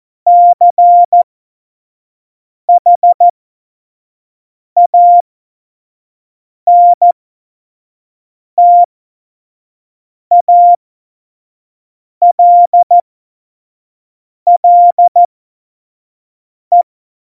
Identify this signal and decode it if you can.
Morse code, heard as a keyed tone, about 14 wpm, spaced as for 5 wpm, 700 Hz. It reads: CHANTALLE
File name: Morsekurs / Witze / Witz 06. Oktober 2020, 25 F70